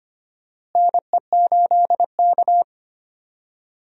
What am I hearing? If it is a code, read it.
Morse code, heard as a keyed tone, about 25 wpm, 700 Hz. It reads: NE8K
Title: USA Call Signs - 4 Characters - Repeated 3x - 25wpm